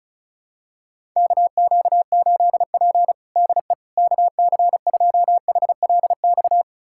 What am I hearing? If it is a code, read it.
Morse code, heard as a keyed tone, about 36 wpm, 700 Hz. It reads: KQ8P DE KC2HLX
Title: USA Callsign DE Callsign 35wpm